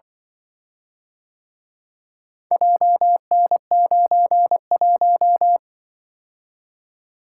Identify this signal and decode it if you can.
Morse code, heard as a keyed tone, about 24 wpm, 700 Hz. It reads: JN91
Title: Speed-Racing - Four Letter-Number Combinations - 20wpm